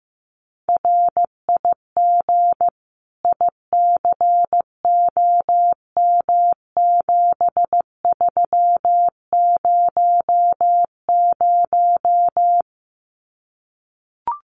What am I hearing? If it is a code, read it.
Morse code, heard as a keyed tone, about 15 wpm, 700 Hz. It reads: RIG ICOM7300
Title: Common QSO Phrases 15wpm